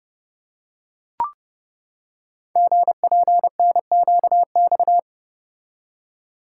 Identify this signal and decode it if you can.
Morse code, heard as a keyed tone, about 30 wpm, 700 Hz. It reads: GPNQX
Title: Code Groups - 5 Letter:Numbers - 30wpm